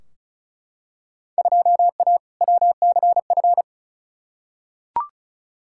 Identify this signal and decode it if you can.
Morse code, heard as a keyed tone, about 34 wpm, 700 Hz. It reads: 2A WCF
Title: Field Day Exchanges - 35wpm